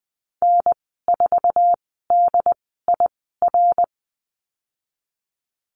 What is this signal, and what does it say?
Morse code, heard as a keyed tone, about 20 wpm, spaced as for 15 wpm, 700 Hz. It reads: N4DIR
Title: 088 Lesson 15g - Number 5 - Call Signs 20wpm